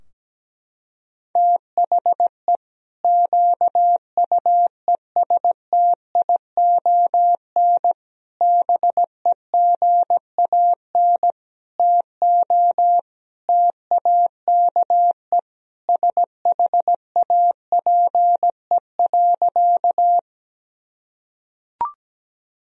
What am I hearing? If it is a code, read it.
Morse code, heard as a keyed tone, about 17 wpm, 700 Hz. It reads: THE QUESTION BEGAN TO TAKE SHAPE.
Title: Sentences from Top 1000 Words - Review Part 3 - 17wpm